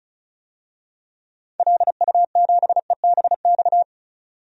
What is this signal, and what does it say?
Morse code, heard as a keyed tone, about 34 wpm, 700 Hz. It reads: LU7EBX
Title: DXCC-100 Call Signs - 6 Characters - 35wpm